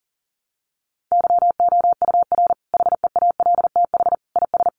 Morse code, heard as a keyed tone, about 40 wpm, 700 Hz. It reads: YOUR HEALTH IS
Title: Sentences from Top 600 Words 40wpm